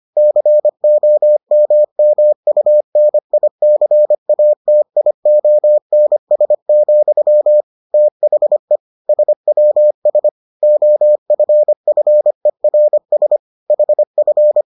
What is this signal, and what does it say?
Morse code, heard as a keyed tone, about 25 wpm, 600 Hz. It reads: COMMUNICATIONS, THE SWS OFFERS HF